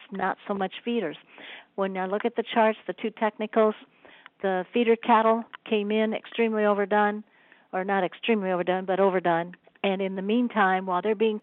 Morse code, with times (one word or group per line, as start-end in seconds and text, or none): none